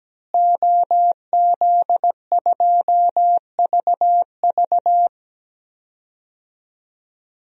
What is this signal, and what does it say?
Morse code, heard as a keyed tone, about 17 wpm, 700 Hz. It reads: OZ2VV
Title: DXCC-100 Call Signs - 5 Characters - Repeated 2x - 17wpm